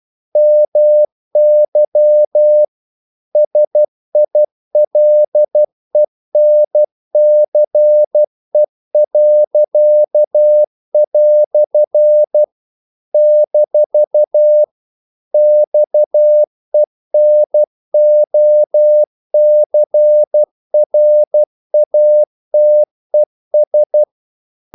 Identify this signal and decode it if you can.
Morse code, heard as a keyed tone, about 12 wpm, 600 Hz. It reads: MY SILENCE." - XENOCRATES
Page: English quotations in CW (telegraphy - morse)